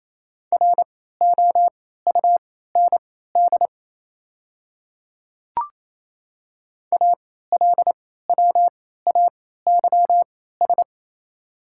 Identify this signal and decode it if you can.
Morse code, heard as a keyed tone, about 27 wpm, spaced as for 16 wpm, 700 Hz. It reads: ROUND ALWAYS
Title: 3x Character Spacing - Top 300 Words - 28wpm